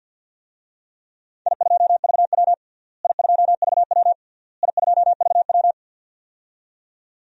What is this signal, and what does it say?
Morse code, heard as a keyed tone, about 52 wpm, 700 Hz. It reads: I2VW I2VW I2VW
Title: DXCC-100 Call Signs - 4 Characters - Repeated 3x - 50wpm